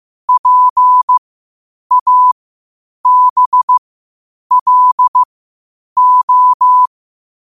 Morse code, heard as a keyed tone, about 15 wpm, spaced as for 8 wpm, 1000 Hz. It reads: PABLO
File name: qotdfiles / 8wpm